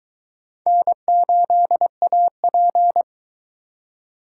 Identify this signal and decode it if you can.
Morse code, heard as a keyed tone, about 22 wpm, 700 Hz. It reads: N8AP